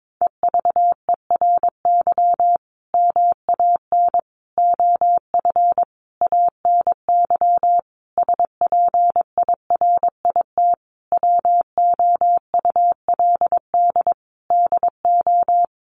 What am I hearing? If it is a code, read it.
Morse code, heard as a keyed tone, about 22 wpm, 700 Hz. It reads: EVERY MAN OF ANY SPIRIT WOULD DO
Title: Sentences from Top 1400 Words 22wpm